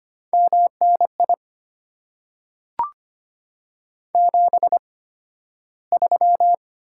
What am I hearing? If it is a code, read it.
Morse code, heard as a keyed tone, about 25 wpm, spaced as for 13 wpm, 690 Hz. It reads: MNI 73
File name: Top 90 QSO Elements - 8x Character Spacing - 25wpm